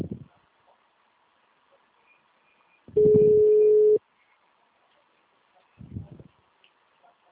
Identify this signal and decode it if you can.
Ringback tone (Europe, 425 Hz).